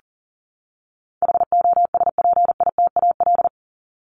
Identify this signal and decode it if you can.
Morse code, heard as a keyed tone, about 40 wpm, 700 Hz. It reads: HOSPITAL
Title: Top 1751-2000 Words 40wpm